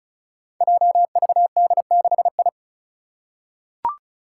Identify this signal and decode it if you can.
Morse code, heard as a keyed tone, about 34 wpm, 700 Hz. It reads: JVD6I
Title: Five Challenging Letter-Number Combinations 35wpm